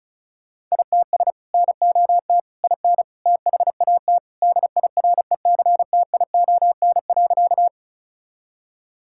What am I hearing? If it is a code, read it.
Morse code, heard as a keyed tone, about 36 wpm, 700 Hz. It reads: ITS NOT IN THAT DIRECTION.
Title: Sentences from Top 1000 Words - Review Part 5 - 35wpm